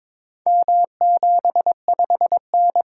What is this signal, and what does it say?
Morse code, heard as a keyed tone, about 22 wpm, 700 Hz. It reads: M75N
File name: practice / 22wpm / Four Letter-Number Combinations - 22wpm